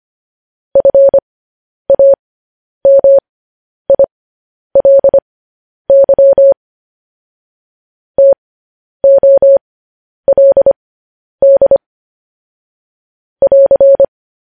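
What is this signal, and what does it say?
Morse code, heard as a keyed tone, about 25 wpm, spaced as for 10 wpm, 550 Hz. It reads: FAMILY TOLD <AR>
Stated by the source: Morse code practice of news headlines at 10 wpm.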